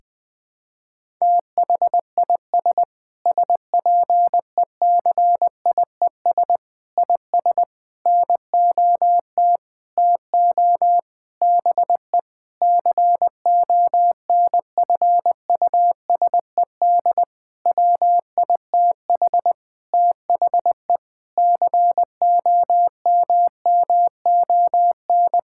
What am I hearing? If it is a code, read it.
Morse code, heard as a keyed tone, about 20 wpm, 700 Hz. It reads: THIS SPECIES IS NOT TO BE CONFUSED WITH THE COMMON
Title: Sentences with 6-Word Phrases - 20wpm